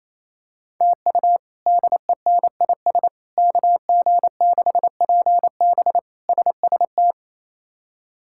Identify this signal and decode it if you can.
Morse code, heard as a keyed tone, about 28 wpm, 700 Hz. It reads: TU DENIS KG6PB SST
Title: K1USN SST QSO Practice - 28wpm